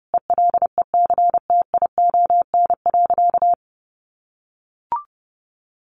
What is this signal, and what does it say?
Morse code, heard as a keyed tone, about 30 wpm, 700 Hz. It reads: ELECTION.
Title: Sentences from Top 1500 Words 30wpm